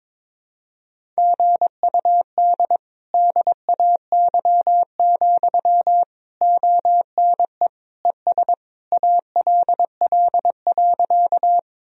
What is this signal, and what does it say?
Morse code, heard as a keyed tone, about 22 wpm, 700 Hz. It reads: GUD DAY, ONE ES ALL.
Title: Sentences from Top 200 Words and Abbreviations 22wpm